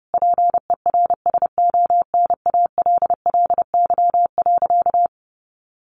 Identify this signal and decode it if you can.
Morse code, heard as a keyed tone, about 30 wpm, 700 Hz. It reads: PERSONALLY.
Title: Sentences from Top 1400 Words 30wpm